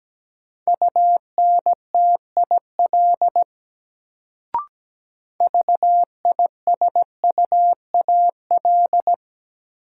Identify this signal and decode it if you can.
Morse code, heard as a keyed tone, about 17 wpm, 700 Hz. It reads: UNTIL VISUAL
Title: Sets of 5 Words Encore - New Word Drill - 17wpm